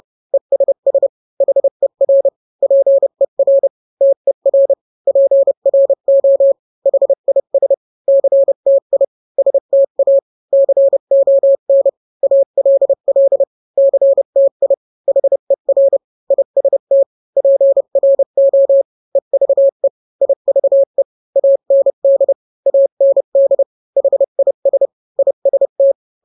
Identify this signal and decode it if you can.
Morse code, heard as a keyed tone, about 30 wpm, 550 Hz. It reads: ESS HER PER TER PRO HIS CTI STA CON ALL CTI HER IST PRO EVE IVE AND AND HIS IST